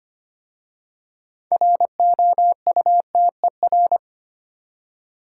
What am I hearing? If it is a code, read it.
Morse code, heard as a keyed tone, about 25 wpm, 700 Hz. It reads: ROUTER
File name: practice / 25wpm / Difficult Words with Hints - 25wpm